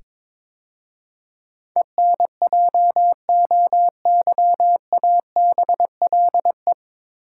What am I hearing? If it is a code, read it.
Morse code, heard as a keyed tone, about 22 wpm, 700 Hz. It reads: ENJOYABLE
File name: 9 Letter Words - First 3 Letters Given - 22wpm